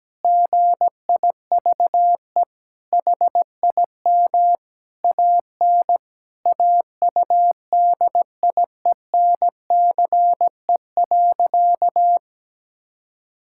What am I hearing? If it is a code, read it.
Morse code, heard as a keyed tone, about 17 wpm, 700 Hz. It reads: GIVE HIM AN AUDIENCE.
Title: Sentences from Top 1000 Words - Review Part 5 - 17wpm